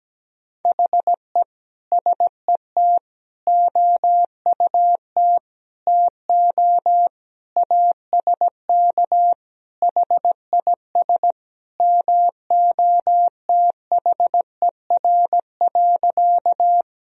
Morse code, heard as a keyed tone, about 17 wpm, 700 Hz. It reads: HE SET OUT TO ASK HIS MOTHER.
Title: Sentences from Top 1000 Words - Review Part 1 - 17wpm